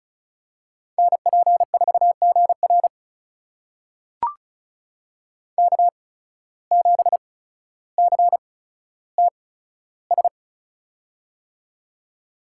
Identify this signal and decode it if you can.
Morse code, heard as a keyed tone, about 36 wpm, spaced as for 15 wpm, 700 Hz. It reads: NP4GR K7CTS
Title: USA Call Signs - 5 Characters - 8x Character Spacing - 35wpm